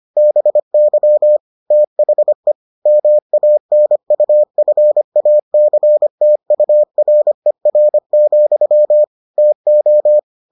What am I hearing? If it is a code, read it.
Morse code, heard as a keyed tone, about 25 wpm, 600 Hz. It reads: BY THE MANUFACTURER, TO